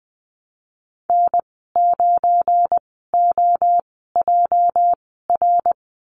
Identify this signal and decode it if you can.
Morse code, heard as a keyed tone, about 19 wpm, spaced as for 14 wpm, 700 Hz. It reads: N9OJR
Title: 219 Lesson 35g - Number 0 - Call Signs 20wpm